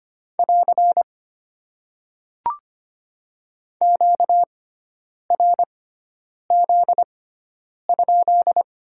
Morse code, heard as a keyed tone, about 25 wpm, spaced as for 9 wpm, 690 Hz. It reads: <AR> QRZ?
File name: Top 90 QSO Elements - 6x Character Spacing - 25wpm